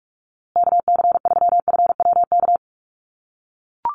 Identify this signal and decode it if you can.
Morse code, heard as a keyed tone, about 44 wpm, 700 Hz. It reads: KC3FWK
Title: USA Call Signs Repeated 2x 45wpm